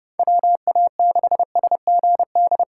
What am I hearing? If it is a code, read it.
Morse code, heard as a keyed tone, about 30 wpm, 700 Hz. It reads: WA6SGD